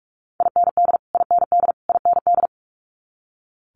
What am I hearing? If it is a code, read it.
Morse code, heard as a keyed tone, about 46 wpm, 700 Hz. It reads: IND IND IND
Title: Multi-letter Phonemes Repeated 3x 45wpm